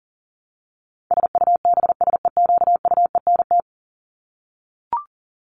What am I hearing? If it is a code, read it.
Morse code, heard as a keyed tone, about 40 wpm, 700 Hz. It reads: SUBSEQUENT
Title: LY Suffix - Top 100 Words with Base Words 40wpm